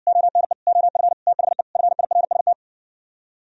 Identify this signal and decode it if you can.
Morse code, heard as a keyed tone, about 59 wpm, 700 Hz. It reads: ONE OF THE FIRST